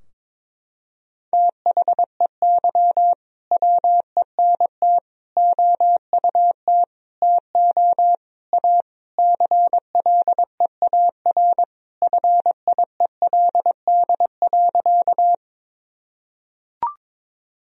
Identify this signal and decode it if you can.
Morse code, heard as a keyed tone, about 22 wpm, 700 Hz. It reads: THEY WENT OUT TO A CLEAR FIELD.
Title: Sentences from Top 1000 Words - Review Part 2 - 22wpm